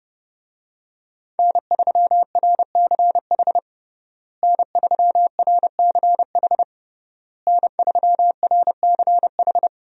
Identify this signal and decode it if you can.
Morse code, heard as a keyed tone, about 30 wpm, 700 Hz. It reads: N3RCH N3RCH N3RCH
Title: USA Call Signs - 5 Characters - Repeated 3x - 30wpm